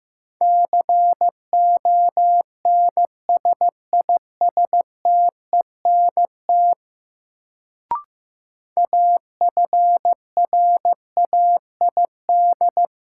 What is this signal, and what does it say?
Morse code, heard as a keyed tone, about 15 wpm, 700 Hz. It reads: CONSISTENT AFRAID
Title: Sentences from Top 1500 Words - New Word Drill - 15wpm